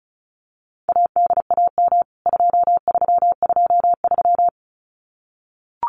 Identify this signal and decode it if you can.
Morse code, heard as a keyed tone, about 34 wpm, 700 Hz. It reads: ADAM 2323
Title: CWT Exchange 35wpm